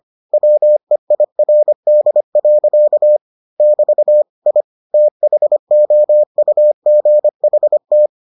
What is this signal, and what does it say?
Morse code, heard as a keyed tone, about 25 wpm, 600 Hz. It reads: WEIRD. <BT> I THOUGHT